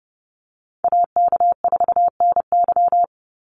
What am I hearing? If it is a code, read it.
Morse code, heard as a keyed tone, about 30 wpm, 700 Hz. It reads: AK4NY
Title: Call Signs Repeated 2x - 30wpm